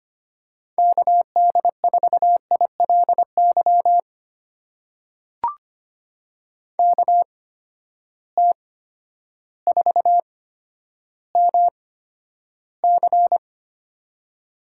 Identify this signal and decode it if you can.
Morse code, heard as a keyed tone, about 25 wpm, spaced as for 11 wpm, 700 Hz. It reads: KD4ILY KT4MC
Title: USA Call Signs - 8x Character Spacing - 25wpm